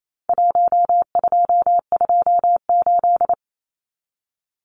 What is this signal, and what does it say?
Morse code, heard as a keyed tone, about 28 wpm, 700 Hz. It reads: 1228
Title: Numbers - Contest Style - 4-Digit 28wpm